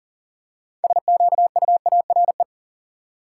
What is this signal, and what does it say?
Morse code, heard as a keyed tone, about 40 wpm, 700 Hz. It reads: SQUARE
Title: Six Letter Words - 40wpm